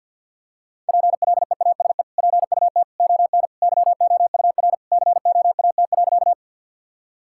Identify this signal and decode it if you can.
Morse code, heard as a keyed tone, about 50 wpm, 700 Hz. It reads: PLEASE PUT ON YOUR COAT.
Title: Sentences from Top 1600 Words 50wpm